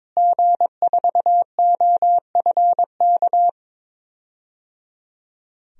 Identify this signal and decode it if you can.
Morse code, heard as a keyed tone, about 22 wpm, 700 Hz. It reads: G4OFK
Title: DXCC-100 Call Signs - 5 Characters - Repeated 3x - 22wpm